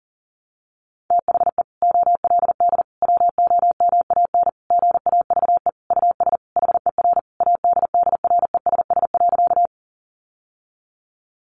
Morse code, heard as a keyed tone, about 40 wpm, 700 Hz. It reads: THE OLD WOMAN GAVE US HER ADDRESS.
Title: Sentences from Top 1100 Words 40wpm